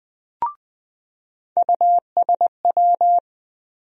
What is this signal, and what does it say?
Morse code, heard as a keyed tone, about 20 wpm, 700 Hz. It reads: USW
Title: USA Call Sign Postfixes - 3 Letters 20wpm